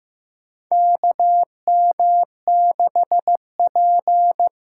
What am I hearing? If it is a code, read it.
Morse code, heard as a keyed tone, about 15 wpm, 700 Hz. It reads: KM6P
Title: USA Call Signs - 6 Characters - 8x Character Spacing - 15wpm